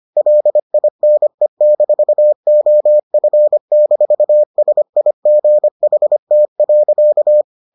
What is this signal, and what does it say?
Morse code, heard as a keyed tone, about 25 wpm, 600 Hz. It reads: LINE-OF-SIGHT.